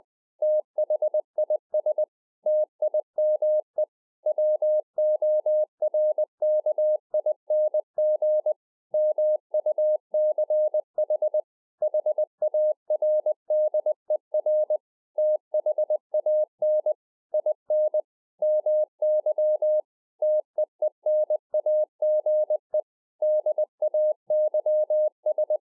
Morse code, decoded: THIS TIME WORKING MUCH HARDER THAN IN MY TEENAGE DAYS

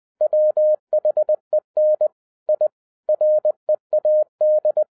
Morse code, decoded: WHEN I READ

